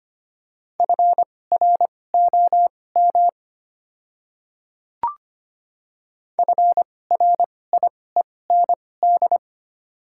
Morse code, heard as a keyed tone, about 25 wpm, spaced as for 18 wpm, 700 Hz. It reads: FROM FRIEND
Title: 2x Character Spacing - Top 300 Words - 25wpm